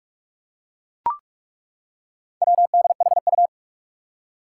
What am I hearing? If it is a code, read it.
Morse code, heard as a keyed tone, about 41 wpm, 700 Hz. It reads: WDHU
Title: Four Challenging Letter-Number Combinations 45wpm